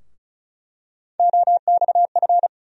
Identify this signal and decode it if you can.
Morse code, heard as a keyed tone, about 34 wpm, 700 Hz. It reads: OXF
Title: USA Call Sign Postfixes - 3 Letters 35wpm